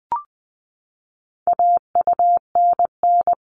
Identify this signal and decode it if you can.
Morse code, heard as a keyed tone, about 20 wpm, 700 Hz. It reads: AUNN
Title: All Cut Numbers - 4-Digit 20wpm